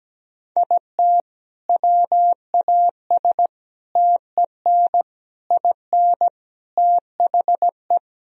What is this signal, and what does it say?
Morse code, heard as a keyed tone, about 17 wpm, 700 Hz. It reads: IT WAS TEN IN THE
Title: Sentences from Top 1000 Words - Review Part 2 - 17wpm